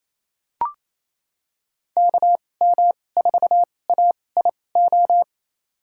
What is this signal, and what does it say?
Morse code, heard as a keyed tone, about 28 wpm, spaced as for 20 wpm, 700 Hz. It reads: KM4AIO
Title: USA Call Signs - 6 Characters - 2x Character Spacing - 28wpm